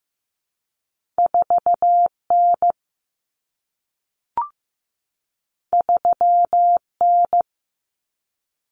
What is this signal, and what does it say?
Morse code, heard as a keyed tone, about 15 wpm, 700 Hz. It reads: 4N 3N
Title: Common Cut Numbers - 2-Digit 15wpm